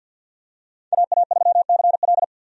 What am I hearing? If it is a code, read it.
Morse code, heard as a keyed tone, about 52 wpm, 700 Hz. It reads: AA3CL
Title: USA Call Signs - 5 Characters - 50wpm